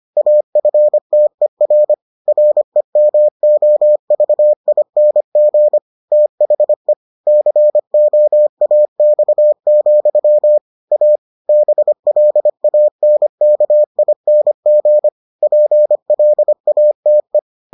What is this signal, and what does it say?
Morse code, heard as a keyed tone, about 25 wpm, 600 Hz. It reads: AFTER REMOVING THE COAX, A BLANKING PLATE